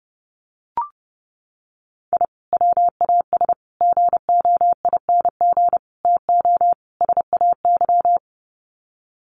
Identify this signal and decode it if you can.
Morse code, heard as a keyed tone, about 30 wpm, 700 Hz. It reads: I WAS GOING TO SAY
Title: Sets of 5 Words 30wpm